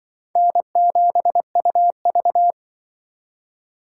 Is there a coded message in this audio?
Yes: Morse code from 0.4 to 2.5 s, heard as a keyed tone, about 26 wpm, 690 Hz. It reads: N7UV